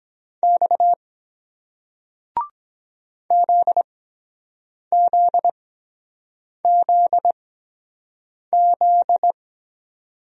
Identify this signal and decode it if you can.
Morse code, heard as a keyed tone, about 23 wpm, spaced as for 7 wpm, 700 Hz. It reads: X ZZZZ